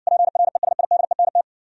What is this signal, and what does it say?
Morse code, heard as a keyed tone, about 60 wpm, 700 Hz. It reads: PRESIDENT